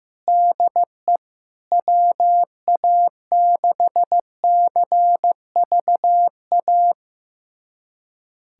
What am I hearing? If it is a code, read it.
Morse code, heard as a keyed tone, about 15 wpm, 700 Hz. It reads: DE WA6CVA
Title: USA Callsign DE Callsign - 6 Characters - 15wpm